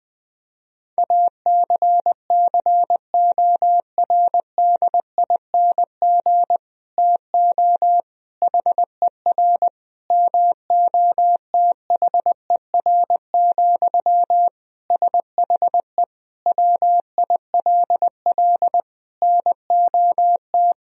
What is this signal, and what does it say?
Morse code, heard as a keyed tone, about 21 wpm, 700 Hz. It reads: ACCORDING TO HER MOTHER, SHE WILL NOT
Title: Sentences from Top 1000 Words - Review Part 4 - 20wpm